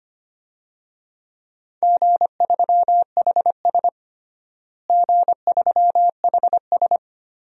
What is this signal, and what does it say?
Morse code, heard as a keyed tone, about 25 wpm, 700 Hz. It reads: G3HS G3HS
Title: DXCC-100 Call Signs - 4 Characters - Repeated 3x - 25wpm